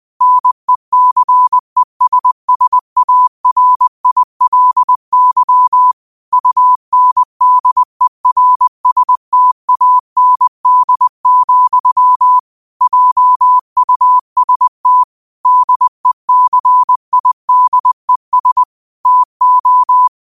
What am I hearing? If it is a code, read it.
Morse code, heard as a keyed tone, about 20 wpm, 1000 Hz. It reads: NECESSARILY UNDERSTAND, JUST DECIDES TO